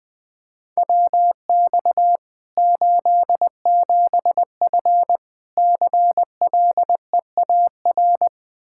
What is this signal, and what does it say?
Morse code, heard as a keyed tone, about 21 wpm, 700 Hz. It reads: WX 87F CLEAR
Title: Practice QSO - Exchange 2 - Info - 20wpm